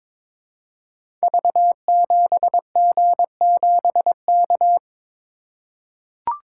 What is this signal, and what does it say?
Morse code, heard as a keyed tone, about 22 wpm, 700 Hz. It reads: V7G7K
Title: Code Groups - 5 Letter:Numbers - 22wpm